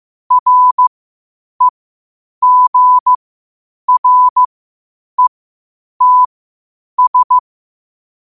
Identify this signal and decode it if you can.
Morse code, heard as a keyed tone, about 15 wpm, spaced as for 8 wpm, 1000 Hz. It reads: REGRETS
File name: qotdfiles / 8wpm